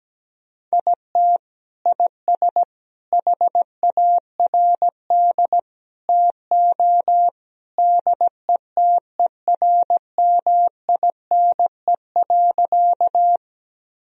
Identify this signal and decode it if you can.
Morse code, heard as a keyed tone, about 17 wpm, 700 Hz. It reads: IT IS HARD TO DETERMINE.
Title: Sentences from Top 1000 Words - Review Part 4 - 17wpm